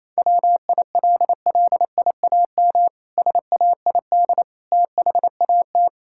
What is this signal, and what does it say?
Morse code, heard as a keyed tone, about 28 wpm, 700 Hz. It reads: WILLIAM SAID THAT
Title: Pangram Challenge v2 - 28wpm